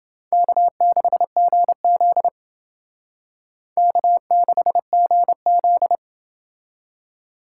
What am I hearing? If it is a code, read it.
Morse code, heard as a keyed tone, about 28 wpm, 700 Hz. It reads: K6GZ K6GZ